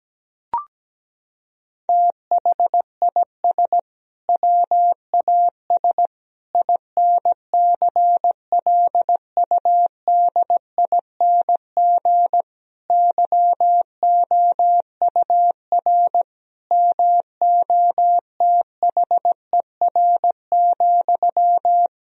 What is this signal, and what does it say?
Morse code, heard as a keyed tone, about 17 wpm, 700 Hz. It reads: THIS WAS INCLUDING YOUR MOTHER,
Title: Sentences from Top 1000 Words - Review Part 4 - 17wpm